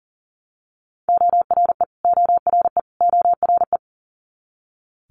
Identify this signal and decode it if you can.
Morse code, heard as a keyed tone, about 40 wpm, 700 Hz. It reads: ORE ORE ORE
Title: Multi-letter Phonemes Repeated 3x 40wpm